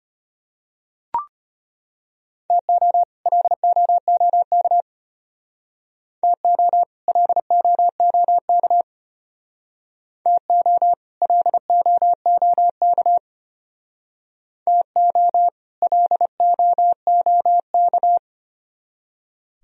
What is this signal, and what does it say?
Morse code, heard as a keyed tone, about 31 wpm, 700 Hz. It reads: TO LOOK TO LOOK TO LOOK TO LOOK
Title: Speed-Racing - Sets of 2 Words 25wpm